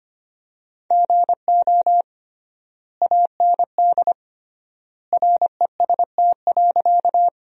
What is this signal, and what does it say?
Morse code, heard as a keyed tone, about 25 wpm, 700 Hz. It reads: GO AND REST.
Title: 3x Word-Spacing - Sentences from Top 500 Words - 25wpm